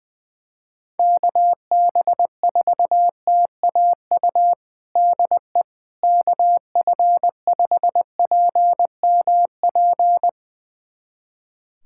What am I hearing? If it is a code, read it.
Morse code, heard as a keyed tone, about 20 wpm, 700 Hz. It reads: KB4TAU DE KF5PMP
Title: USA Callsign DE Callsign - 6 Characters - 20wpm